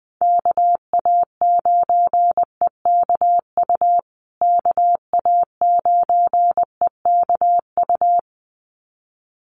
Morse code, heard as a keyed tone, about 20 wpm, 700 Hz. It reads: KA9EKU KA9EKU
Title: USA Call Signs Repeated 2x 20wpm